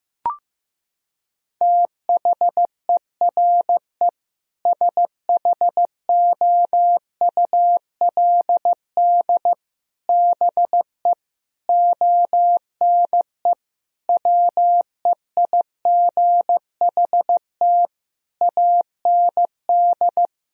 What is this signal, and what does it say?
Morse code, heard as a keyed tone, about 16 wpm, 700 Hz. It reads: THERE SHOULD BE ONE WEIGHT AND
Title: Sentences from Top 1000 Words - Review Part 3 - 15wpm